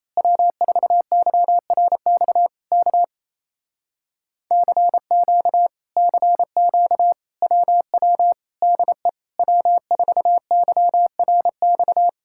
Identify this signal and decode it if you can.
Morse code, heard as a keyed tone, about 27 wpm, 700 Hz. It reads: W4YRX K CQ CQ WW DE W4YRX
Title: Speed-Racing Common QSO Phrases - 28wpm